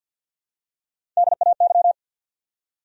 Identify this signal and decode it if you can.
Morse code, heard as a keyed tone, about 47 wpm, 700 Hz. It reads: DAY
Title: Top 100 Words 50wpm